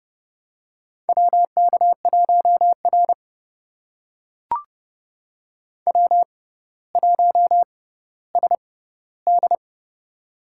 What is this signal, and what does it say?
Morse code, heard as a keyed tone, about 30 wpm, spaced as for 15 wpm, 700 Hz. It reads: WK1R W1SD
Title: USA Call Signs - 6x Character Spacing - 30wpm